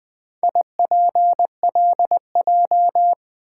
Morse code, heard as a keyed tone, about 20 wpm, 700 Hz. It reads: IPLJ
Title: Four Letter-Number Combinations - 20wpm